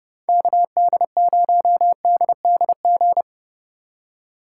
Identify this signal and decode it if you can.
Morse code, heard as a keyed tone, about 36 wpm, 700 Hz. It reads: KD0DDG
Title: Speed Racing - USA 2x3 Call Signs 30wpm